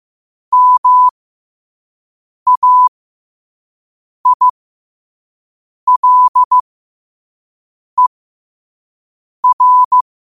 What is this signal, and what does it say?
Morse code, heard as a keyed tone, about 15 wpm, spaced as for 5 wpm, 1000 Hz. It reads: MAILER
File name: qotdfiles / 5wpm